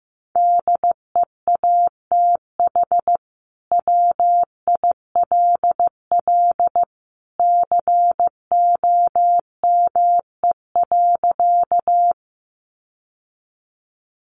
Morse code, heard as a keyed tone, about 15 wpm, 700 Hz. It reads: DEATH WILL COME.
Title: Sentences from Top 1100 Words 15wpm